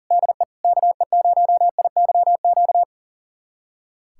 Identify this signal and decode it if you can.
Morse code, heard as a keyed tone, about 40 wpm, 700 Hz. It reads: DE KE0IYQ
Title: USA Callsign DE Callsign - 6 Characters - 40wpm